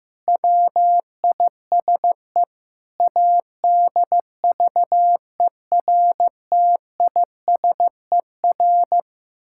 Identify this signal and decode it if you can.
Morse code, heard as a keyed tone, about 16 wpm, 700 Hz. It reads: WISE ADVERTISER
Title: Pangram Challenge v2 - 15wpm